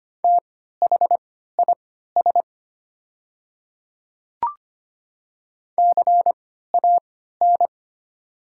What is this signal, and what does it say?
Morse code, heard as a keyed tone, about 25 wpm, spaced as for 14 wpm, 690 Hz. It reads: THIS CAN
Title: 3x Character Spacing - Top 200 Words - 25wpm